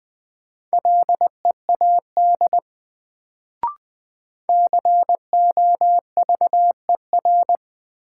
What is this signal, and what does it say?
Morse code, heard as a keyed tone, about 20 wpm, 700 Hz. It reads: LEAD COVER
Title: Sentences from Top 500 Words - New Word Drill - 20wpm